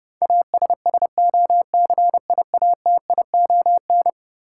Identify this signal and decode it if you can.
Morse code, heard as a keyed tone, about 32 wpm, 700 Hz. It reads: ASSOCIATION
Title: Sentences from Top 1100 Words - New Word Drill Alt - 30wpm